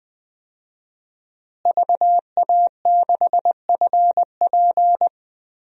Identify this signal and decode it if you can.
Morse code, heard as a keyed tone, about 20 wpm, 700 Hz. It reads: VA6FP
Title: DXCC-100 Call Signs - 5 Characters - Repeated 2x - 20wpm